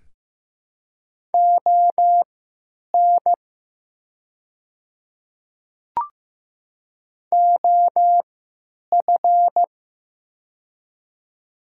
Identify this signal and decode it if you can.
Morse code, heard as a keyed tone, about 15 wpm, spaced as for 9 wpm, 700 Hz. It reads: ON OF
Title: 3x Character Spacing - Top 200 Words - 15wpm